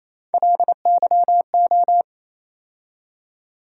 Morse code, heard as a keyed tone, about 28 wpm, 700 Hz. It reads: LYO
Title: USA Call Sign Postfixes - 3 Letters 28wpm